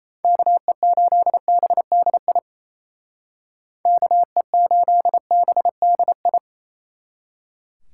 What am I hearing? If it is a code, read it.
Morse code, heard as a keyed tone, about 34 wpm, 700 Hz. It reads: KE8BDI KE8BDI
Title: Speed Racing - USA Call Signs 28wpm